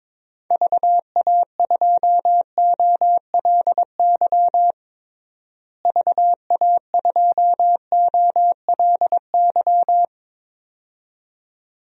Morse code, heard as a keyed tone, about 22 wpm, 700 Hz. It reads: VA2OLY VA2OLY
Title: DXCC-100 Call Signs - 6 Characters - Repeated 2x - 22wpm